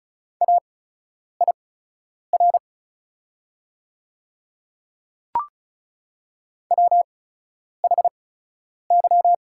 Morse code, heard as a keyed tone, about 34 wpm, spaced as for 10 wpm, 690 Hz. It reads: AIR WHY
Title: 8x Character Spacing - Top 200 Words - 35wpm